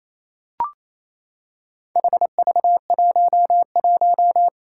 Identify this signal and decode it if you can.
Morse code, heard as a keyed tone, about 28 wpm, 700 Hz. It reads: HV11
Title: Four Challenging Letter-Number Combinations 28wpm